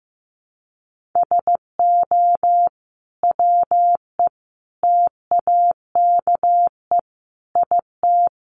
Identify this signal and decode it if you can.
Morse code, heard as a keyed tone, about 15 wpm, 700 Hz. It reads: SO WE TAKE IT
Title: Sentences from Top 700 Words 15wpm